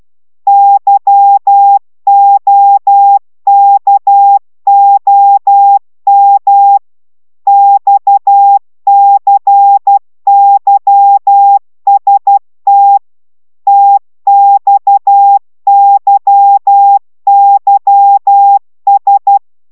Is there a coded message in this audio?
Yes: Morse code from 0.5 to 19.4 s, heard as a keyed tone, about 12 wpm, 800 Hz. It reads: YOKOM XCYST TXYYS